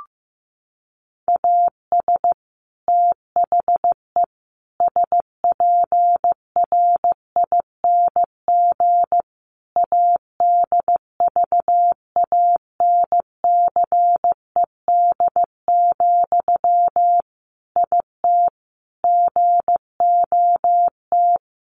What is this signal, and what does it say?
Morse code, heard as a keyed tone, about 15 wpm, 700 Hz. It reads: AS THE SPRING ADVANCED, IT GOT
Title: Sentences from Top 1200 Words 15wpm